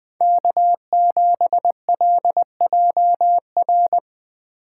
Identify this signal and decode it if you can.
Morse code, heard as a keyed tone, about 19 wpm, 700 Hz. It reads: K7LJR